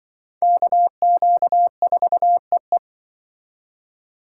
Morse code, heard as a keyed tone, about 24 wpm, 700 Hz. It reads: KQ4EE
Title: Speed Racing - USA 2x2 Call Signs 20wpm